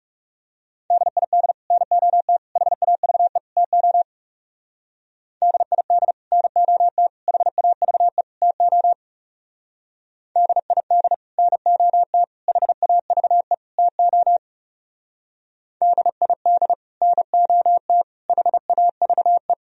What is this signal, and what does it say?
Morse code, heard as a keyed tone, about 39 wpm, 700 Hz. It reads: DID NOT HAVE TO DID NOT HAVE TO DID NOT HAVE TO DID NOT HAVE